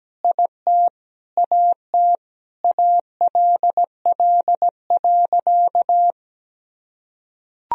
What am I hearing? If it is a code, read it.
Morse code, heard as a keyed tone, about 17 wpm, 700 Hz. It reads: IT AT ALL.
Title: Sentences from Top 1000 Words - Review Part 1 - 17wpm